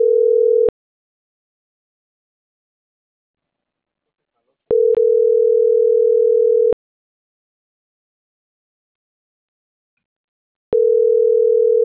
Ringback tone (North America), 440 + 480 Hz.